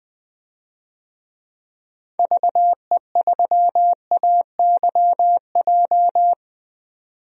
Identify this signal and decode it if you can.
Morse code, heard as a keyed tone, about 20 wpm, 700 Hz. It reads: VE3AYJ
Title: DXCC-100 Call Signs - 6 Characters - 20wpm